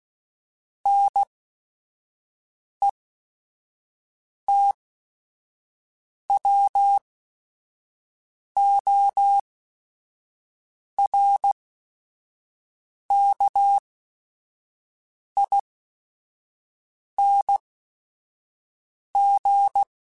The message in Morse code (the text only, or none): NETWORKING